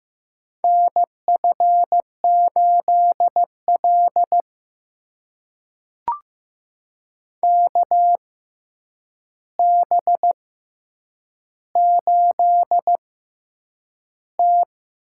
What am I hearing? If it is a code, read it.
Morse code, heard as a keyed tone, about 15 wpm, spaced as for 8 wpm, 700 Hz. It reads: NF8L KB8T
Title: USA Call Signs - 4 Characters - 6x Character Spacing - 15wpm